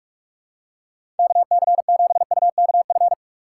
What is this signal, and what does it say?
Morse code, heard as a keyed tone, about 46 wpm, 700 Hz. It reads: KC7UKF
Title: USA Call Signs - 2x Character Spacing - 45wpm